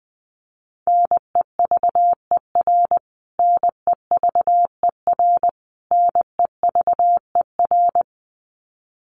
Morse code, heard as a keyed tone, about 20 wpm, 700 Hz. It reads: NEVER NEVER NEVER
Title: Top 500 Words Repeated 20wpm